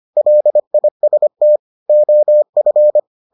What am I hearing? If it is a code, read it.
Morse code, heard as a keyed tone, about 25 wpm, 600 Hz. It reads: LIST OF